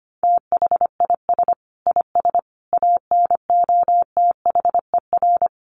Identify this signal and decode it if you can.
Morse code, heard as a keyed tone, about 25 wpm, 700 Hz. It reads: THIS IS ANOTHER
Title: Sentences from Top 1400 Words 25wpm